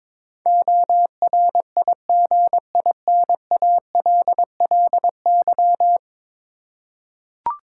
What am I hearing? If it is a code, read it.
Morse code, heard as a keyed tone, about 22 wpm, 700 Hz. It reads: ORIGINALLY
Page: morse-code-ninja-practice-22wpm